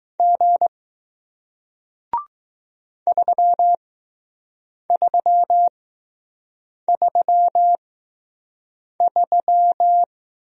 Morse code, heard as a keyed tone, about 20 wpm, spaced as for 6 wpm, 700 Hz. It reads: G 3333